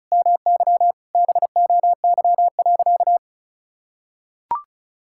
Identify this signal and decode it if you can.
Morse code, heard as a keyed tone, about 34 wpm, 700 Hz. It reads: MY BOY.
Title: Sentences from Top 1000 Words - Review Part 6 - 35wpm